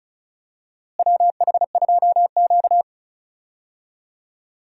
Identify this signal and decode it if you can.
Morse code, heard as a keyed tone, about 34 wpm, 700 Hz. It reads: WH2Q
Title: USA Call Signs - 4 Characters - Repeated 3x - 35wpm